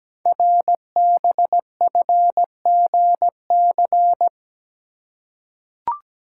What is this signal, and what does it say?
Morse code, heard as a keyed tone, about 17 wpm, 700 Hz. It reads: RBFGC
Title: Code Groups - 5 Letters - 17wpm